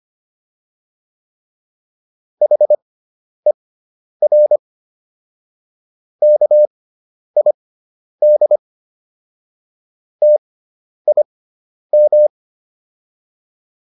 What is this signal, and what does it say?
Morse code, heard as a keyed tone, about 25 wpm, spaced as for 10 wpm, 600 Hz. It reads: HER KID TIM